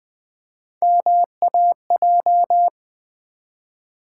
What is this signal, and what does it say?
Morse code, heard as a keyed tone, about 20 wpm, 690 Hz. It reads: MAJ
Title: Build A Word - 1 Letter At A Time - 5-Letter Words - 20wpm